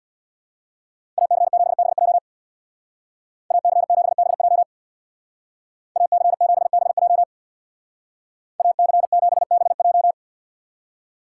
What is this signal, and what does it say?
Morse code, heard as a keyed tone, about 62 wpm, 700 Hz. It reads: AC7BJ AC7BJ AC7BJ AC7BJ
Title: Speed Racing - USA Call Signs 50wpm